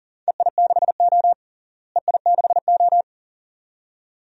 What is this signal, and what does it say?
Morse code, heard as a keyed tone, about 37 wpm, 700 Hz. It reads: EI6O EI6O
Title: DXCC-100 Call Signs - 4 Characters - Repeated 2x - 40wpm